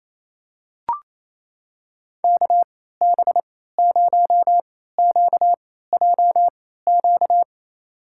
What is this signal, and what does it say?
Morse code, heard as a keyed tone, about 28 wpm, spaced as for 16 wpm, 700 Hz. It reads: KB0QJQ
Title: USA Call Signs - 6 Characters - 3x Character Spacing - 28wpm